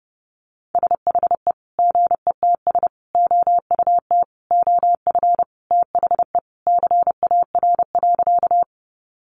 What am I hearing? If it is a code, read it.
Morse code, heard as a keyed tone, about 30 wpm, 700 Hz. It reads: SHE GETS OUT OF THE CAR.
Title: Easy Phrases 30wpm